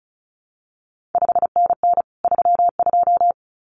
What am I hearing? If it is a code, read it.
Morse code, heard as a keyed tone, about 36 wpm, 700 Hz. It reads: 5NN 32
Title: CQ WW DX Contest Exchange - 35wpm